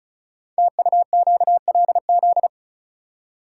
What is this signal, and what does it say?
Morse code, heard as a keyed tone, about 36 wpm, 700 Hz. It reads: TUQLZ
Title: Code Groups - 5 Letters - 35wpm